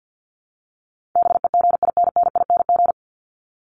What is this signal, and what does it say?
Morse code, heard as a keyed tone, about 49 wpm, 700 Hz. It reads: BEGINNING
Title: ING Suffix - Top 100 Words with Base Words 50wpm